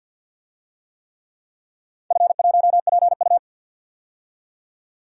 Morse code, heard as a keyed tone, about 53 wpm, 700 Hz. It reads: F1PU
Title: DXCC-100 Call Signs - 4 Characters - 50wpm